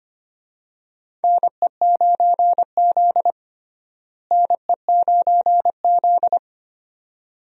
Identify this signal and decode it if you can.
Morse code, heard as a keyed tone, about 25 wpm, 700 Hz. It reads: NE9Z NE9Z
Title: USA Call Signs - 4 Characters - Repeated 2x - 25wpm